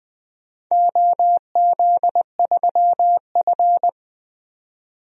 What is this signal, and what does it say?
Morse code, heard as a keyed tone, about 20 wpm, 700 Hz. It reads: OZ3F
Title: DXCC-100 Call Signs - 4 Characters - 20wpm